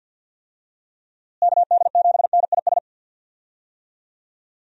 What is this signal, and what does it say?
Morse code, heard as a keyed tone, about 47 wpm, 700 Hz. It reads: KD7NIS
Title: USA Call Signs - 6 Characters - Repeated 2x - 50wpm